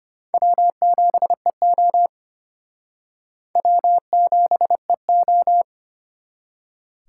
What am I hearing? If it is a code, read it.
Morse code, heard as a keyed tone, about 31 wpm, 700 Hz. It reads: W7EO W7EO
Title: Speed Racing - USA 1x2 Call Signs 25wpm